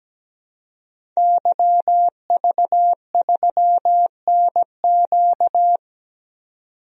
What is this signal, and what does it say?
Morse code, heard as a keyed tone, about 17 wpm, 700 Hz. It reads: YV3NQ
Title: DXCC-100 Call Signs - 5 Characters - 17wpm